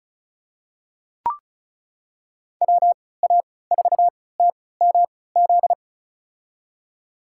Morse code, heard as a keyed tone, about 34 wpm, spaced as for 20 wpm, 700 Hz. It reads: WA4TMZ
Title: USA Call Signs - 3x Character Spacing - 35wpm